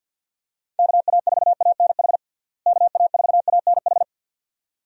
Morse code, heard as a keyed tone, about 48 wpm, 700 Hz. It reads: KA4ANH KA4ANH
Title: USA Call Signs - 6 Characters - Repeated 2x - 50wpm